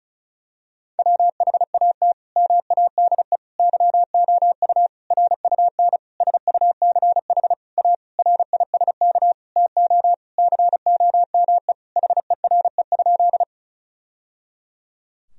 Morse code, heard as a keyed tone, about 36 wpm, 700 Hz. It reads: WHAT MADE YOU RUN SUCH A RISK TO COME HERE?
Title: Sentences from Top 1000 Words - Review Part 4 - 35wpm